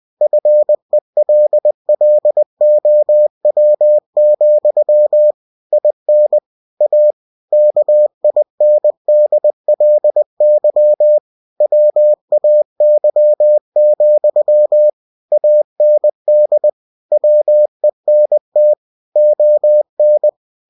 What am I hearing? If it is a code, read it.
Morse code, heard as a keyed tone, about 20 wpm, 600 Hz. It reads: FELLOW, IN A KINDLY WAY, AND WENT ON
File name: cw-books / winnie-the-pooh / winnie-20wpm-page-0106